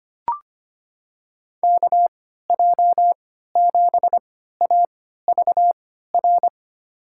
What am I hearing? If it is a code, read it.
Morse code, heard as a keyed tone, about 25 wpm, spaced as for 14 wpm, 700 Hz. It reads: KJ7AVR
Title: USA Call Signs - 6 Characters - 3x Character Spacing - 25wpm